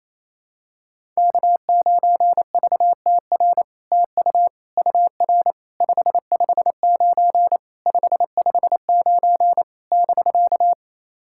Morse code, heard as a keyed tone, about 28 wpm, 700 Hz. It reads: K9VTR TU UR 559 559 <BK>
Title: SOTA QSO Practice 28wpm